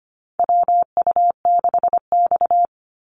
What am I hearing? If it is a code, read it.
Morse code, heard as a keyed tone, about 25 wpm, 700 Hz. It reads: WU6X
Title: Call Signs - 25wpm